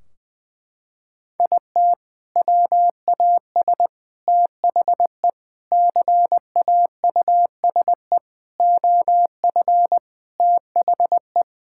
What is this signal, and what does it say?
Morse code, heard as a keyed tone, about 20 wpm, 700 Hz. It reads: IT WAS THE CAUSE OF THE
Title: Sentences from Top 1600 Words 20wpm